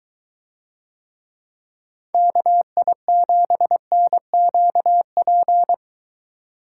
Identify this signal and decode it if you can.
Morse code, heard as a keyed tone, about 23 wpm, 700 Hz. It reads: KI7NQP